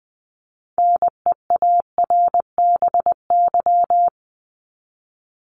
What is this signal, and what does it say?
Morse code, heard as a keyed tone, about 20 wpm, 700 Hz. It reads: NEARBY
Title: Top 1501-1750 Words 20wpm